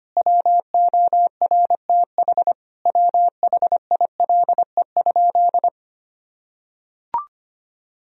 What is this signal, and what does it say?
Morse code, heard as a keyed tone, about 25 wpm, 700 Hz. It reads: WORTH WHILE?
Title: Sentences from Top 1000 Words - Review Part 1 - 25wpm